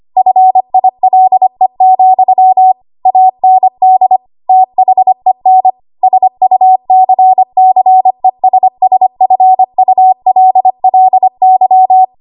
Morse code, heard as a keyed tone, about 25 wpm, 750 Hz. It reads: FILE, AND THEN SUCCESSFULLY
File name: Archive / 25 WPM / 25WPM